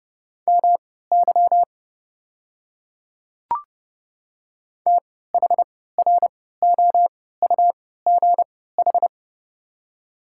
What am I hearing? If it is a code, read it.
Morse code, heard as a keyed tone, about 30 wpm, spaced as for 17 wpm, 700 Hz. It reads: MY THROUGH
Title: 3x Character Spacing - Top 200 Words - 30wpm